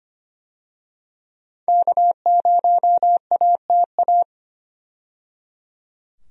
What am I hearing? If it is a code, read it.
Morse code, heard as a keyed tone, about 25 wpm, 700 Hz. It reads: K0ATA
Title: USA Call Signs - 5 Characters - 25wpm